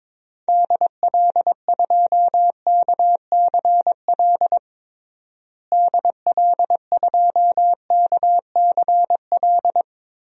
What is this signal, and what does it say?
Morse code, heard as a keyed tone, about 22 wpm, 700 Hz. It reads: DL2KCL DL2KCL
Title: DXCC-100 Call Signs - 6 Characters - Repeated 3x - 22wpm